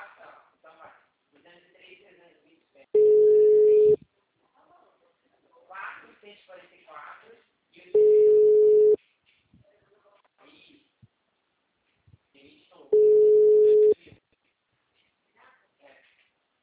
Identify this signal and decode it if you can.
Ringback tone (Europe, 425 Hz), with speech.
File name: out- -2012- - - .22357